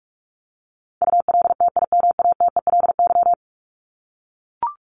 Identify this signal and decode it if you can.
Morse code, heard as a keyed tone, about 44 wpm, 700 Hz. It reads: ULTIMATELY
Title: LY Suffix - Top 100 Words with Base Words 45wpm